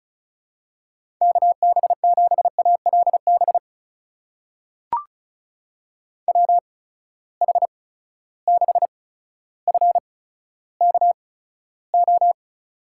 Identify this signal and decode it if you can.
Morse code, heard as a keyed tone, about 36 wpm, spaced as for 15 wpm, 700 Hz. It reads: KB7ALB WH6FKO
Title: USA Call Signs - 6 Characters - 8x Character Spacing - 35wpm